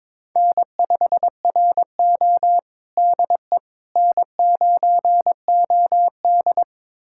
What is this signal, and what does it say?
Morse code, heard as a keyed tone, about 22 wpm, 700 Hz. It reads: N5RO DE N9OD
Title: USA Callsign DE Callsign - 4 Characters - 22wpm